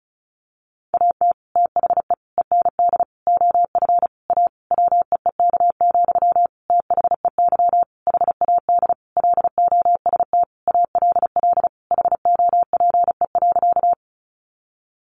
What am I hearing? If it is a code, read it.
Morse code, heard as a keyed tone, about 34 wpm, 700 Hz. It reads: AT THE END OF A WEEK, THEY HAD LOST ALL HOPE.
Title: Sentences from Top 1400 Words 35wpm